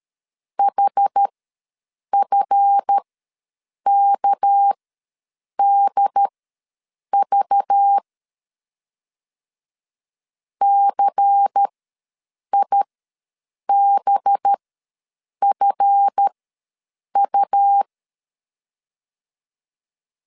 Morse code, heard as a keyed tone, about 13 wpm, spaced as for 7 wpm, 800 Hz. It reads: HFKDV CIBFU
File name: LEZIONI / 8 YJ45 / 8 YJ4 ws 12wpm 3249